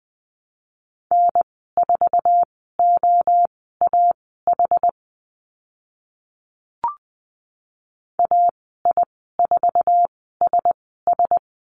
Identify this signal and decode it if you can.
Morse code, heard as a keyed tone, about 20 wpm, spaced as for 14 wpm, 700 Hz. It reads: N4OAH AI4SS
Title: 062 Lesson 11g - Letter h - Call Signs 20wpm